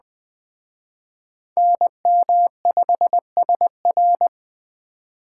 Morse code, heard as a keyed tone, about 20 wpm, 700 Hz. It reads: NM5SR